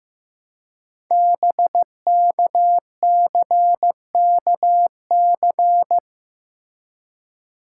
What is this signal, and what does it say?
Morse code, heard as a keyed tone, about 15 wpm, 700 Hz. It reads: BKCKC
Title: Code Groups - 5 Letters - 15wpm